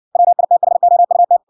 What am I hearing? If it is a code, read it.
Morse code, heard as a keyed tone, about 56 wpm, 700 Hz. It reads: WITHOUT